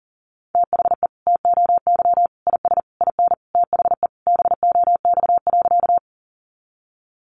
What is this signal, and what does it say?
Morse code, heard as a keyed tone, about 40 wpm, 700 Hz. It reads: THE TOY IS IN THE BOX.
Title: Easy Phrases 40wpm